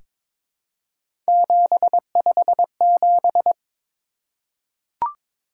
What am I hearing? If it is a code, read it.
Morse code, heard as a keyed tone, about 22 wpm, 700 Hz. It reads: 757
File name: practice / 22wpm / Three Letter-Number Combinations - 22wpm